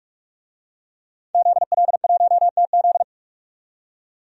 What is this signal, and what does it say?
Morse code, heard as a keyed tone, about 43 wpm, 700 Hz. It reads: ZL1TZ
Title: DXCC-100 Call Signs - 5 Characters - 45wpm